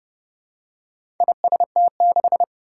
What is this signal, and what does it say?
Morse code, heard as a keyed tone, about 30 wpm, 700 Hz. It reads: IST6
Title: Four Challenging Letter-Number Combinations 30wpm